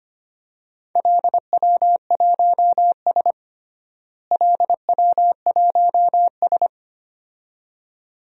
Morse code, heard as a keyed tone, about 25 wpm, 700 Hz. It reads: LW1S LW1S
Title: DXCC-100 Call Signs - 4 Characters - Repeated 2x - 25wpm